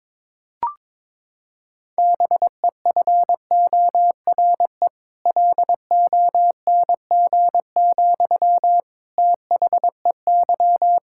Morse code, heard as a keyed tone, about 22 wpm, 700 Hz. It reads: BEFORE LONG, THEY
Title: Sentences from Top 1000 Words - Review Part 6 - 22wpm